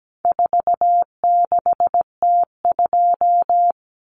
Morse code, heard as a keyed tone, about 17 wpm, 700 Hz. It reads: 46T2
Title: Common Cut Numbers - 4-Digit 17wpm